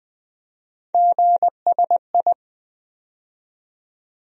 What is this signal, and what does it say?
Morse code, heard as a keyed tone, about 20 wpm, 700 Hz. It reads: GSI
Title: USA Call Sign Postfixes - 3 Letters 20wpm